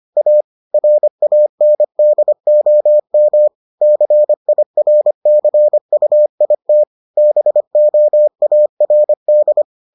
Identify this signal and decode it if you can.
Morse code, heard as a keyed tone, about 25 wpm, 600 Hz. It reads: A RANDOM CIRCUIT BOARD